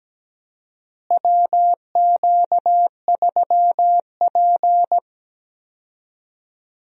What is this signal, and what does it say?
Morse code, heard as a keyed tone, about 17 wpm, 700 Hz. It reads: WQ3P
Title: USA Call Signs - 4 Characters - Repeated 3x - 17wpm